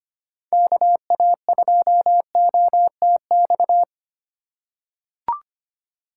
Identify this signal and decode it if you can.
Morse code, heard as a keyed tone, about 25 wpm, 700 Hz. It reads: KA2OTX
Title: USA Call Signs - 2x Character Spacing - 25wpm